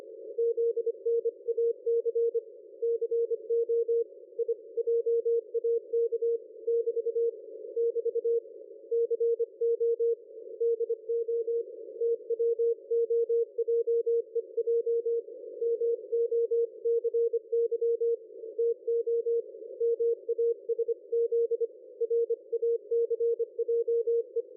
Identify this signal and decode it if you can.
Morse code, heard as a keyed tone, about 25 wpm, 460 Hz. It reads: ZNAC CO I JAK <BT> <BT> CO DO TWOJEJ MOCY TO MASZ RACJE
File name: dlazaawansowanych / nagraniazpasma / ExpertSDR2 line out 23.01.2020 14.04.01